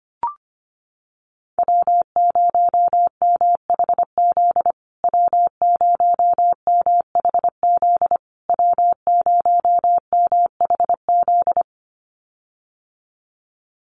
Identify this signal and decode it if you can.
Morse code, heard as a keyed tone, about 25 wpm, 700 Hz. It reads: W0MHZ W0MHZ W0MHZ
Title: USA Call Signs Repeated 3x 25wpm